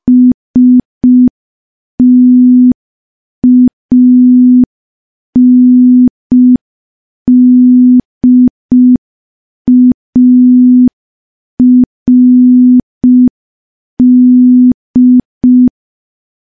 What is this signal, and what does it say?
Morse code, heard as a keyed tone, about 5 wpm, 250 Hz. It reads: STANDARD
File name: morse / morse-training-05WPM